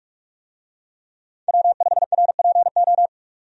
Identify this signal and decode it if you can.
Morse code, heard as a keyed tone, about 46 wpm, 700 Hz. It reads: W5RPO
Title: USA Call Signs - 5 Characters - 45wpm